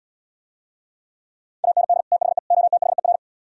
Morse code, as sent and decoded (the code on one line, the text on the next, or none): .- -. -.. / - .... / --- - .... .-.
AND THE OTHER